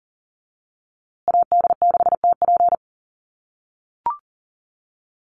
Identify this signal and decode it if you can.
Morse code, heard as a keyed tone, about 40 wpm, 700 Hz. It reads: AD6TP
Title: USA Call Signs Repeated 3x 40wpm